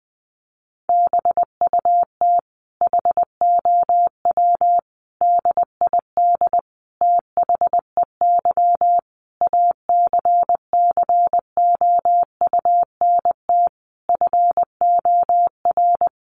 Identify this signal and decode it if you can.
Morse code, heard as a keyed tone, about 20 wpm, 700 Hz. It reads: BUT HOW DID THEY ACCOUNT FOR
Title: Sentences from Top 900 Words 20wpm